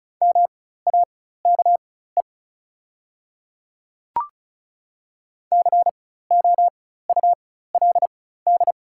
Morse code, heard as a keyed tone, about 34 wpm, spaced as for 16 wpm, 700 Hz. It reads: MAKE COULD
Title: 4x Character Spacing - Top 100 Words - 35wpm